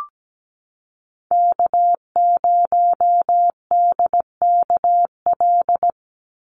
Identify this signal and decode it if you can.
Morse code, heard as a keyed tone, about 17 wpm, 700 Hz. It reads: K0DKL
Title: USA Call Signs 17wpm